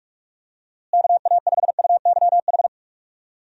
Morse code, heard as a keyed tone, about 42 wpm, 700 Hz. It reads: KA5UYH
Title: USA Call Signs - 6 Characters - Repeated 3x - 45wpm